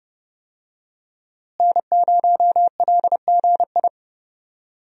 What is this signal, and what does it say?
Morse code, heard as a keyed tone, about 30 wpm, 700 Hz. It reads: N0LGI
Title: USA Call Signs - 5 Characters - Repeated 2x - 30wpm